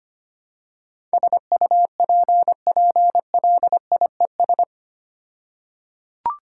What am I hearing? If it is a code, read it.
Morse code, heard as a keyed tone, about 25 wpm, 700 Hz. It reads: SUPPLIES
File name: practice / 25wpm / All-Age Appropriate Jokes - 25wpm